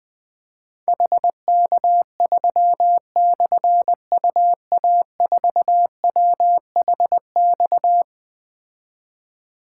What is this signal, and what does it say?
Morse code, heard as a keyed tone, about 20 wpm, 700 Hz. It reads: HK3/UA4WHX
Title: Difficult Call Signs 20wpm